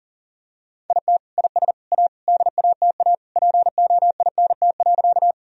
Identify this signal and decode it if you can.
Morse code, heard as a keyed tone, about 37 wpm, 700 Hz. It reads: IT IS A DATA POINT.
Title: Sentences from Top 1000 Words - Review Part 3 - 40wpm